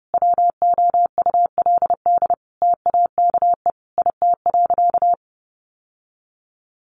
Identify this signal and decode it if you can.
Morse code, heard as a keyed tone, about 30 wpm, 700 Hz. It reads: WOULD TAKE IT.
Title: Sentences from Top 700 Words 30wpm